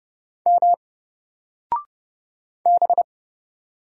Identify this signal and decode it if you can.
Morse code, heard as a keyed tone, about 30 wpm, 700 Hz. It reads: M B
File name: Single-Letter-Number-30wpm